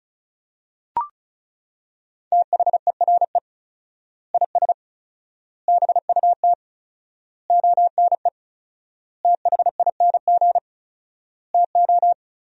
Morse code, heard as a keyed tone, about 36 wpm, 700 Hz. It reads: THERE IS BUT ONE THING TO
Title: 4x Word-Spacing - Sentences from Top 100 Words - 35wpm